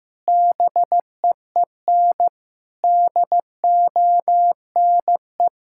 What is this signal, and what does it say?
Morse code, heard as a keyed tone, about 15 wpm, 700 Hz. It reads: BEEN DONE
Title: Sentences from Top 1600 Words 15wpm